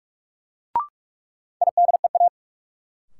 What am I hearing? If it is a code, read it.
Morse code, heard as a keyed tone, about 46 wpm, 700 Hz. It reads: IDEA
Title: Sentences from Top 300 Words - New Word Drill - 45wpm